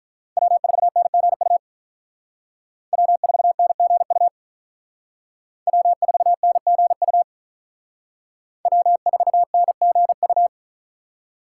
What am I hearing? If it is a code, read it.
Morse code, heard as a keyed tone, about 43 wpm, 700 Hz. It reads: W4NGU W4NGU W4NGU W4NGU
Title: Speed Racing - USA 1x3 Call Signs 35wpm